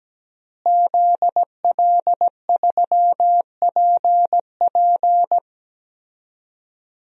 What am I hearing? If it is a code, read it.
Morse code, heard as a keyed tone, about 17 wpm, 700 Hz. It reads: ZL3PP
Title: DXCC-100 Call Signs - 5 Characters - Repeated 2x - 17wpm